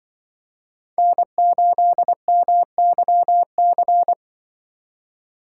Morse code, heard as a keyed tone, about 25 wpm, 700 Hz. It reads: N8MYC
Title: Speed Racing - USA Call Signs 20wpm